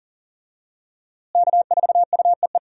Morse code, heard as a keyed tone, about 40 wpm, 700 Hz. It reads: K4UEE
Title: USA Call Signs - 5 Characters - Repeated 3x - 40wpm